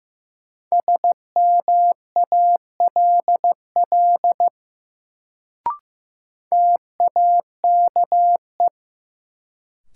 Morse code, heard as a keyed tone, about 15 wpm, 700 Hz. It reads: SMALL TAKE
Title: Sets of 2 Words - New Word Drill - 15wpm